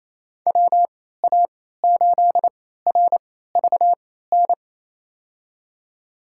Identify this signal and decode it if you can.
Morse code, heard as a keyed tone, about 28 wpm, spaced as for 16 wpm, 700 Hz. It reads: WA8RVN
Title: USA Call Signs - 6 Characters - 3x Character Spacing - 28wpm